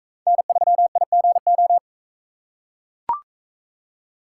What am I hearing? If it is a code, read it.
Morse code, heard as a keyed tone, about 41 wpm, 700 Hz. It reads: N3IGO